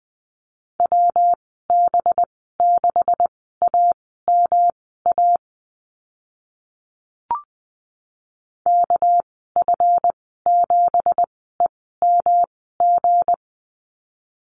Morse code, heard as a keyed tone, about 20 wpm, spaced as for 14 wpm, 700 Hz. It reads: WB6AMA KF7EMG
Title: 219 Lesson 35g - Number 0 - Call Signs 20wpm